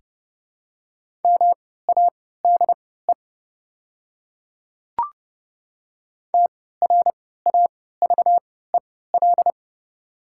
Morse code, heard as a keyed tone, about 29 wpm, spaced as for 17 wpm, 700 Hz. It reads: MADE TRAVEL
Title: 3x Character Spacing - Top 500 Words - 30wpm